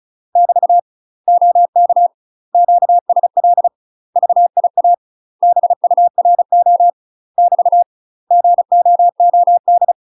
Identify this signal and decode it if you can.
Morse code, heard as a keyed tone, about 36 wpm, 700 Hz. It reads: <BT> OK QSL VIA BURO <BT> GOOD
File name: UK MASTER QSOs TRAINING / 35WPM / UK MASTER QSOs 0013